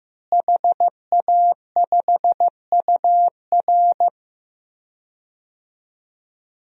Morse code, heard as a keyed tone, about 16 wpm, 700 Hz. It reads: HA5UR
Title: DXCC-100 Call Signs - 5 Characters - Repeated 2x - 15wpm